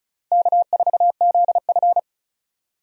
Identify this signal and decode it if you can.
Morse code, heard as a keyed tone, about 36 wpm, 700 Hz. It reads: K4ZF
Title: USA Call Signs - 4 Characters - 35wpm